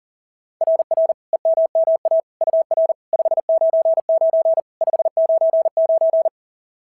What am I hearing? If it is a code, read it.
Morse code, heard as a keyed tone, about 40 wpm, 650 Hz. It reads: RR EMMA UR 599 599
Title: SOTA QSO Practice 40wpm